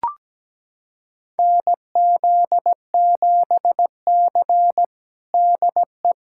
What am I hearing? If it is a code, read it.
Morse code, heard as a keyed tone, about 17 wpm, 700 Hz. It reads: NZ7C DE
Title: USA Callsign DE Callsign - 4 Characters - 17wpm